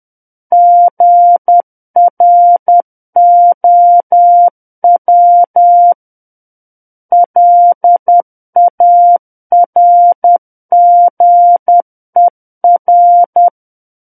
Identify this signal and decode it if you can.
Morse code, heard as a keyed tone, about 10 wpm, 700 Hz. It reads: GROW LARGER
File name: Book One / 01-Chapter 1 - 10 WPM